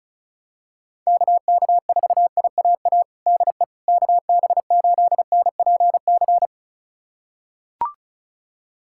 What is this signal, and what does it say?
Morse code, heard as a keyed tone, about 36 wpm, 700 Hz. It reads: KK4IAA DE KB8NPC
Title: USA Callsign DE Callsign - 6 Characters - 35wpm